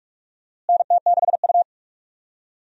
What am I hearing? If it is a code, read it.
Morse code, heard as a keyed tone, about 47 wpm, 700 Hz. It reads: NT6U
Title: USA Call Signs - 4 Characters - Repeated 2x - 45wpm